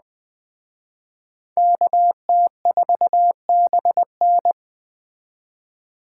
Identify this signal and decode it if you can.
Morse code, heard as a keyed tone, about 20 wpm, 700 Hz. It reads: KT4BN